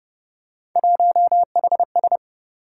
Morse code, heard as a keyed tone, about 32 wpm, 700 Hz. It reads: 1HS